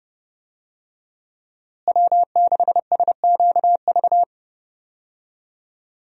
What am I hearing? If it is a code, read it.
Morse code, heard as a keyed tone, about 30 wpm, 700 Hz. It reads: W6SQV
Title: USA Call Signs - 5 Characters - Repeated 3x - 30wpm